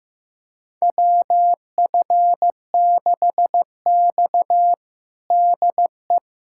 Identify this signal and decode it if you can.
Morse code, heard as a keyed tone, about 15 wpm, 700 Hz. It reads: WF6X DE
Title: USA Callsign DE Callsign - 4 Characters - 15wpm